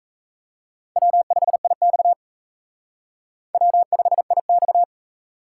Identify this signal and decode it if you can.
Morse code, heard as a keyed tone, about 34 wpm, 700 Hz. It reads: W5IX W5IX
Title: Speed Racing - USA 1x2 Call Signs 28wpm